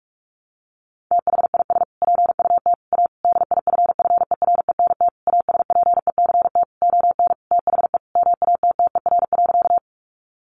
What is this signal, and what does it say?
Morse code, heard as a keyed tone, about 44 wpm, 700 Hz. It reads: THIS PUT A DIFFERENT ASPECT ON THE MATTER.
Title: Sentences from Top 1300 Words 45wpm